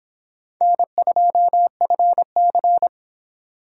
Morse code, heard as a keyed tone, about 26 wpm, 700 Hz. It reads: N2FC